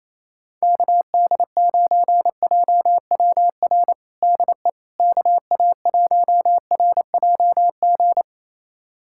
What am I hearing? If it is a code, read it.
Morse code, heard as a keyed tone, about 28 wpm, 700 Hz. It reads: KD9JWR DE KA1RJG
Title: USA Callsign DE Callsign 28wpm